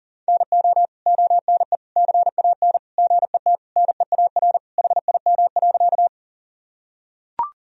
Morse code, heard as a keyed tone, about 40 wpm, 700 Hz. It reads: NO ONE CAN GET NEAR HIM.
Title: Sentences from Top 1000 Words - Review Part 1 - 40wpm